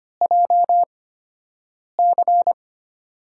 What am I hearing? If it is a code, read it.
Morse code, heard as a keyed tone, about 25 wpm, spaced as for 7 wpm, 700 Hz. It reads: JC
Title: USA Call Signs - 8x Character Spacing - 25wpm